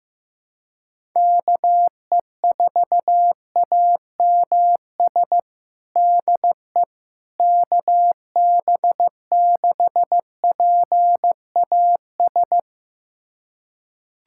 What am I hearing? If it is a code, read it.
Morse code, heard as a keyed tone, about 15 wpm, 700 Hz. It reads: KE4AMS DE KB6PAS
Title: USA Callsign DE Callsign 15wpm